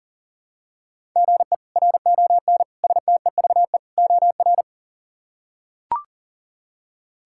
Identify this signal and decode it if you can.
Morse code, heard as a keyed tone, about 40 wpm, 700 Hz. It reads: GE RON STEVE OR
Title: K1USN SST QSO Practice - 40wpm